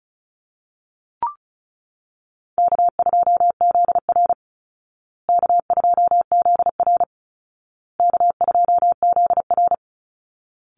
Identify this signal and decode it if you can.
Morse code, heard as a keyed tone, about 34 wpm, 700 Hz. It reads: K2ZR K2ZR K2ZR
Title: Call Signs Repeated 2x - 35wpm